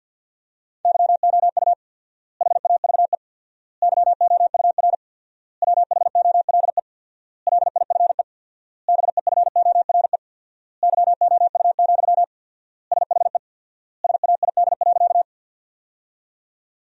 Morse code, heard as a keyed tone, about 50 wpm, 700 Hz. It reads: YOU HAVE YOUR WHOLE LIFE BEFORE YOU, SHE SAID.
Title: 4x Word-Spacing - Sentences from Top 400 Words - 50wpm